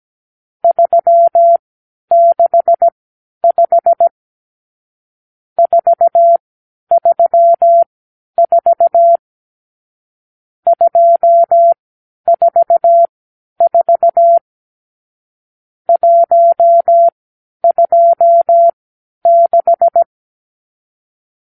Morse code, heard as a keyed tone, about 17 wpm, spaced as for 10 wpm, 680 Hz. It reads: 365 434 244 126
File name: k7qo / Lesson 50 (1-6)